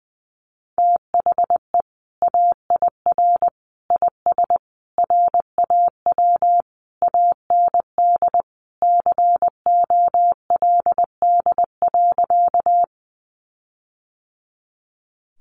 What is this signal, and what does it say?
Morse code, heard as a keyed tone, about 20 wpm, 700 Hz. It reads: THE AIR IS RAW AND COLD.
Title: Sentences from Top 1500 Words 20wpm